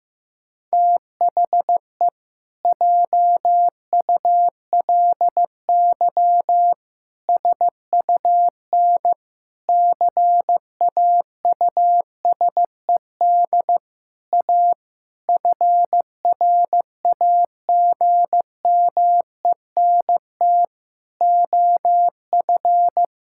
Morse code, heard as a keyed tone, about 16 wpm, 700 Hz. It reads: THE JULY SUN CAUSED A FRAGMENT OF
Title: Pangram Challenge v2 - 15wpm